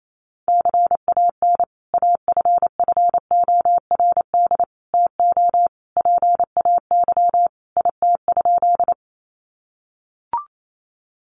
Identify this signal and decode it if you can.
Morse code, heard as a keyed tone, about 28 wpm, 700 Hz. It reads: CAN AFFORD TO PAY IT?
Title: Sentences from Top 1100 Words 28wpm